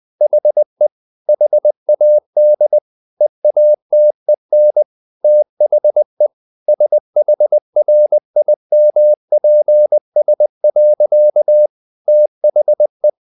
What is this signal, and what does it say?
Morse code, heard as a keyed tone, about 20 wpm, 600 Hz. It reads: HE HAD EATEN THE SHRIMPS. THE
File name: the old man and the sea part 2 20wpm